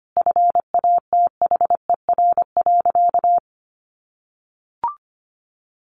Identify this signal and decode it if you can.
Morse code, heard as a keyed tone, about 25 wpm, 700 Hz. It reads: FATHER.
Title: Sentences from Top 500 Words 25wpm